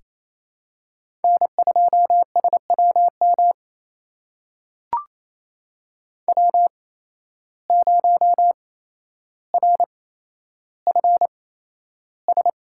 Morse code, heard as a keyed tone, about 28 wpm, spaced as for 12 wpm, 700 Hz. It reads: N2SWM W0RFS
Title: USA Call Signs - 5 Characters - 8x Character Spacing - 28wpm